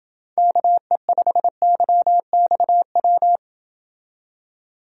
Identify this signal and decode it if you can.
Morse code, heard as a keyed tone, about 29 wpm, 700 Hz. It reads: KE5YXW